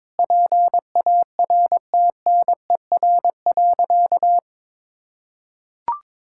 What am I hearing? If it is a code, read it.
Morse code, heard as a keyed tone, about 22 wpm, 700 Hz. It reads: PARTNER.
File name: practice / 22wpm / Sentences with 8-Word Phrases - 22wpm